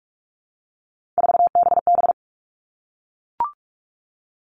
Morse code, heard as a keyed tone, about 44 wpm, 700 Hz. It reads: 4BB
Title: All Cut Numbers - 3-Digit 45wpm